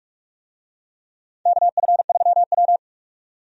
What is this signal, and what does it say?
Morse code, heard as a keyed tone, about 46 wpm, 700 Hz. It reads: KF3W
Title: USA Call Signs - 4 Characters - 45wpm